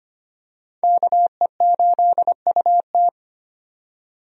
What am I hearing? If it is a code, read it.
Morse code, heard as a keyed tone, about 25 wpm, 700 Hz. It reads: KE8UT
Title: USA Call Signs - 5 Characters - 25wpm